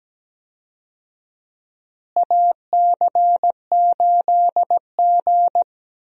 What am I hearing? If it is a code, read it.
Morse code, heard as a keyed tone, about 17 wpm, 700 Hz. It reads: AC8G
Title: USA Call Signs - 4 Characters - Repeated 2x - 17wpm